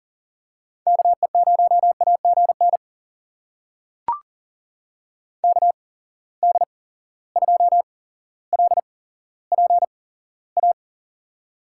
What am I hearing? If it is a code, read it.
Morse code, heard as a keyed tone, about 40 wpm, spaced as for 17 wpm, 700 Hz. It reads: KE0AGN KD2LPA
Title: USA Call Signs - 6 Characters - 8x Character Spacing - 40wpm